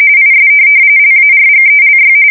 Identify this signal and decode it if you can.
RTTY, about 45.45 baud. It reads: IS LARGELY
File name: fsk170 / printing-telegraph